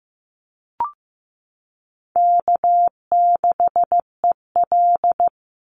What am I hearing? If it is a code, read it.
Morse code, heard as a keyed tone, about 15 wpm, 700 Hz. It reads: K6EL
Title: NA SOTA Chaser Callsigns 15wpm